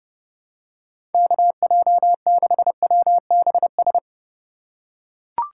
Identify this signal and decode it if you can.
Morse code, heard as a keyed tone, about 32 wpm, 700 Hz. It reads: KJ6WBS
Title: USA Call Signs - 3x Character Spacing - 30wpm